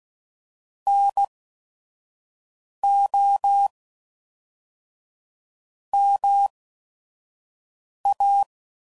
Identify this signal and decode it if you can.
Morse code, heard as a keyed tone, about 16 wpm, spaced as for 5 wpm, 800 Hz. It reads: NO MA
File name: CW Code Groups @5WPM